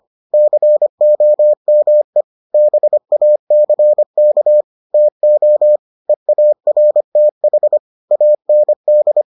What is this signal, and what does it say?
Morse code, heard as a keyed tone, about 25 wpm, 600 Hz. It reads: COME BACK TO EARTH AND